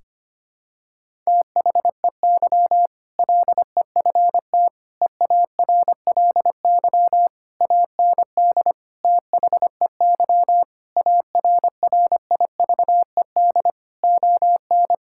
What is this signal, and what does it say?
Morse code, heard as a keyed tone, about 25 wpm, 700 Hz. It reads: THEY LEFT EARLY AND THEY ARRIVED ON
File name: practice / 25wpm / Sentences with 2-Word Phrases - 25wpm